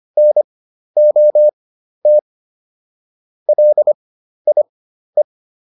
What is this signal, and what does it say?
Morse code, heard as a keyed tone, about 25 wpm, spaced as for 12 wpm, 600 Hz. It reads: NOT LIE